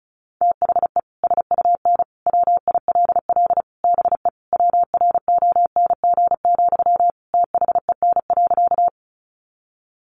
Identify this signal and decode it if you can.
Morse code, heard as a keyed tone, about 34 wpm, 700 Hz. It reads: THE SUN WILL BE WRONG, THEN.
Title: Sentences from Top 900 Words 35wpm